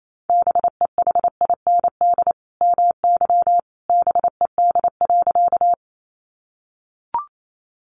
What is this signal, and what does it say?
Morse code, heard as a keyed tone, about 28 wpm, 700 Hz. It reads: BEHIND MY BED.
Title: Sentences from Top 500 Words 28wpm